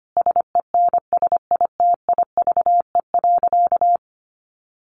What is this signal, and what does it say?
Morse code, heard as a keyed tone, about 25 wpm, 700 Hz. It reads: SENSITIVE.
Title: Sentences from Top 1400 Words 25wpm